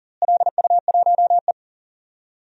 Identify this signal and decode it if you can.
Morse code, heard as a keyed tone, about 40 wpm, 700 Hz. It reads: LU1E
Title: DXCC-100 Call Signs - 4 Characters - Repeated 3x - 40wpm